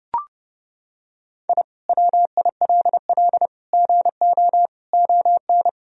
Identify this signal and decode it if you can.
Morse code, heard as a keyed tone, about 30 wpm, 700 Hz. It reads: I WILL GO ON
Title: Sentences from Top 1000 Words - Review Part 3 - 30wpm